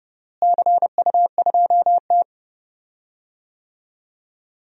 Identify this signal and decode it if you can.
Morse code, heard as a keyed tone, about 30 wpm, 700 Hz. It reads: CU2T
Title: DXCC-100 Call Signs - 4 Characters - Repeated 3x - 30wpm